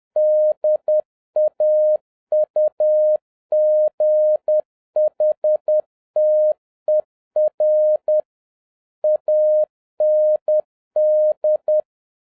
Morse code, heard as a keyed tone, about 10 wpm, 610 Hz. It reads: DAUGHTER AND